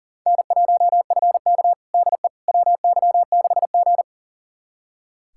Morse code, heard as a keyed tone, about 37 wpm, 700 Hz. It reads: N1FK DE WY6G
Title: USA Callsign DE Callsign - 4 Characters - 40wpm